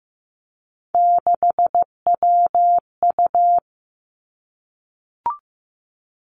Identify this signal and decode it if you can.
Morse code, heard as a keyed tone, about 15 wpm, 700 Hz. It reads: 6WU
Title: Three Challenging Letter-Number Combinations 15wpm